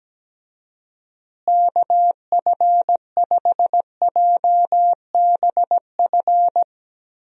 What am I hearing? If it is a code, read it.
Morse code, heard as a keyed tone, about 17 wpm, 700 Hz. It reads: KF5JBF
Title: USA Call Signs - 6 Characters - Repeated 2x - 17wpm